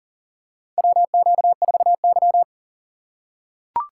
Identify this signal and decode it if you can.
Morse code, heard as a keyed tone, about 40 wpm, 690 Hz. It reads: WQ4Y
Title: USA Call Signs - 4 Characters - 2x Character Spacing - 40wpm